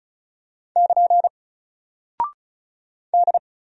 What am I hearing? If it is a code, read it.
Morse code, heard as a keyed tone, about 35 wpm, 700 Hz. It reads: <KN> D
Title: Single Character (Letter:Number:Punctuation:Prosign) 35wpm